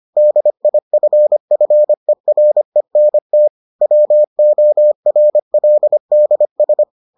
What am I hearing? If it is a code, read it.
Morse code, heard as a keyed tone, about 25 wpm, 600 Hz. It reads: DIFFERENT WORLDS